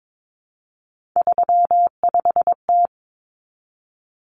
Morse code, heard as a keyed tone, about 22 wpm, 700 Hz. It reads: 35T
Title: Common Cut Numbers - 3-Digit 22wpm